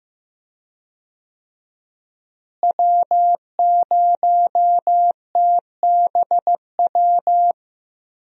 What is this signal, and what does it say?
Morse code, heard as a keyed tone, about 15 wpm, 700 Hz. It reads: W0TBW
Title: USA Call Signs - 5 Characters - Repeated 3x - 15wpm